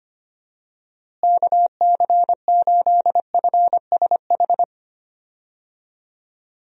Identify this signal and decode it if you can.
Morse code, heard as a keyed tone, about 25 wpm, 700 Hz. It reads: KC8FSH
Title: USA Call Signs - 6 Characters - Repeated 2x - 25wpm